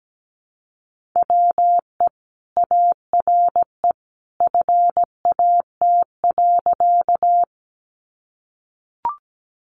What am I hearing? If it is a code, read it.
Morse code, heard as a keyed tone, about 17 wpm, 700 Hz. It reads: WE ARE FAT.
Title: Sentences from Top 600 Words 17wpm